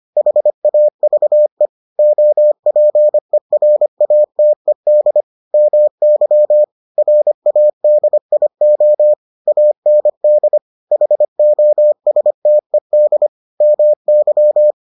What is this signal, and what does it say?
Morse code, heard as a keyed tone, about 25 wpm, 600 Hz. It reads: HAVE OPERATED MY RADIO AND HOSTED MY